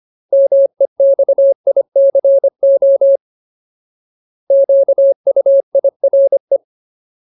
Morse code, heard as a keyed tone, about 25 wpm, 550 Hz. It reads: MEXICO QUIRE
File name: Ultimate Word List / 25WPM Ultimate